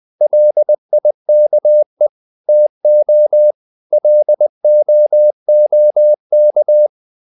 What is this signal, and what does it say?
Morse code, heard as a keyed tone, about 20 wpm, 600 Hz. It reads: LIKE TO LOOK
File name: the old man and the sea part 2 20wpm